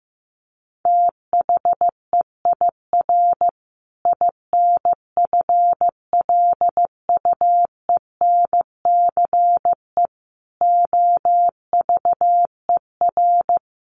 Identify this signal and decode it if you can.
Morse code, heard as a keyed tone, about 15 wpm, 700 Hz. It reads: THEIR INFLUENCE OVER
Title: Sentences from Top 1400 Words 15wpm